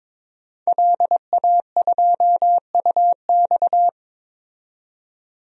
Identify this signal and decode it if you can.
Morse code, heard as a keyed tone, about 22 wpm, 700 Hz. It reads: LA2UX
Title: DXCC-100 Call Signs - 5 Characters - 22wpm